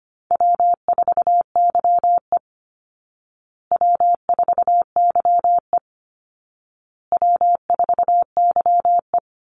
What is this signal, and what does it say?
Morse code, heard as a keyed tone, about 25 wpm, 700 Hz. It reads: W4YE W4YE W4YE
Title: Call Signs Repeated 2x - 25wpm